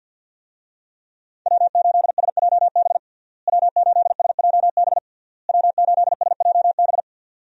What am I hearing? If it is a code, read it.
Morse code, heard as a keyed tone, about 51 wpm, 700 Hz. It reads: W8SJB W8SJB W8SJB
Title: USA Call Signs - 5 Characters - Repeated 3x - 50wpm